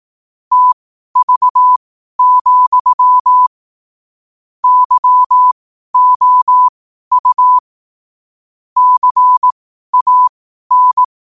Morse code, heard as a keyed tone, about 18 wpm, spaced as for 12 wpm, 1000 Hz. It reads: TV, YOU CAN
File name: qotdfiles / 12wpm